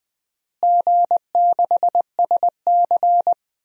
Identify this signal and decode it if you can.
Morse code, heard as a keyed tone, about 20 wpm, 700 Hz. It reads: G6SC
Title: DXCC-100 Call Signs - 4 Characters - Repeated 3x - 20wpm